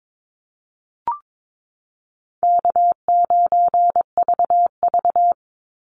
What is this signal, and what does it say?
Morse code, heard as a keyed tone, about 22 wpm, 700 Hz. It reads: K9VV
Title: Call Signs - 22wpm